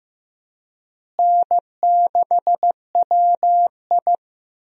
Morse code, heard as a keyed tone, about 15 wpm, 700 Hz. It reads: N6WI
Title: USA Call Signs - 4 Characters - Repeated 3x - 15wpm